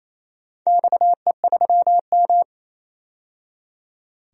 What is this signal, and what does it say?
Morse code, heard as a keyed tone, about 29 wpm, 700 Hz. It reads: XE3M
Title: DXCC-100 Call Signs - 4 Characters - 28wpm